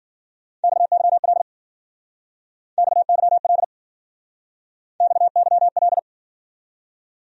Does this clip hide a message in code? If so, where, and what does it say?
Yes: Morse code from 0.6 to 6.0 s, heard as a keyed tone, about 60 wpm, 700 Hz. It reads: XYL XYL XYL